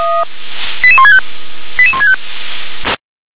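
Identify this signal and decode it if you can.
Five-tone selective call (ZVEI), heard as quick runs of five tones, tones of about 80 ms. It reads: 88165 88165
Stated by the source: ZVEI five-tone selective call 88126 received on an SDR.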